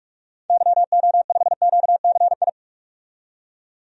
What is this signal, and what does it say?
Morse code, heard as a keyed tone, about 45 wpm, 690 Hz. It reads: YO5QCI
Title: DXCC-100 Call Signs - 6 Characters - 45wpm